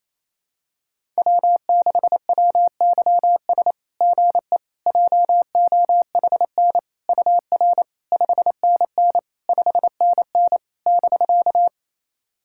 Morse code, heard as a keyed tone, about 28 wpm, 700 Hz. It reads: W6WYS GE JOHN UR 5NN 5NN <BK>
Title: SOTA QSO Practice 28wpm